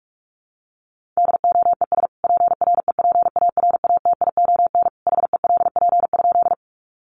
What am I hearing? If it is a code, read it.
Morse code, heard as a keyed tone, about 44 wpm, 700 Hz. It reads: DOES PREPARATION HELP?
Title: Sentences from Top 1400 Words 45wpm